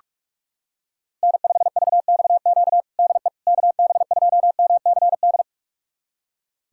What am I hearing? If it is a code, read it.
Morse code, heard as a keyed tone, about 41 wpm, 700 Hz. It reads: N5VXQ DE KB2MCD
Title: USA Callsign DE Callsign 45wpm